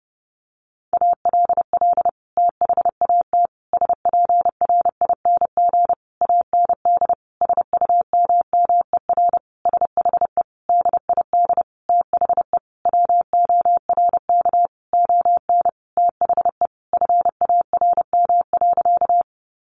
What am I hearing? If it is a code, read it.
Morse code, heard as a keyed tone, about 28 wpm, 700 Hz. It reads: ALL THAT SPRING AND SUMMER SHE DID THE WORK ON THE FARM.
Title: Sentences from Top 1200 Words 30wpm